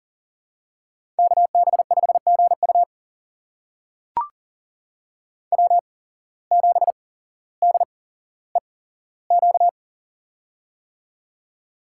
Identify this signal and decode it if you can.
Morse code, heard as a keyed tone, about 40 wpm, spaced as for 17 wpm, 700 Hz. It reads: KB5GU W7DEQ
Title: USA Call Signs - 5 Characters - 8x Character Spacing - 40wpm